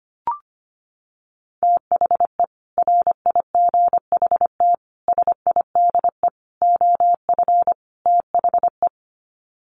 Morse code, heard as a keyed tone, about 25 wpm, 700 Hz. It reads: THE RIGHT SIDE OF THE
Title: Sets of 5 Words Encore 25wpm